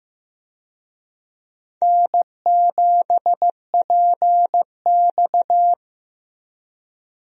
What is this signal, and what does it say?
Morse code, heard as a keyed tone, about 15 wpm, 700 Hz. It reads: N7PX
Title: USA Call Signs - 4 Characters - Repeated 2x - 15wpm